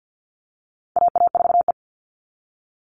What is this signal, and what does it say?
Morse code, heard as a keyed tone, about 50 wpm, 700 Hz. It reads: AA4E
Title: All Cut Numbers - 4-Digit 50wpm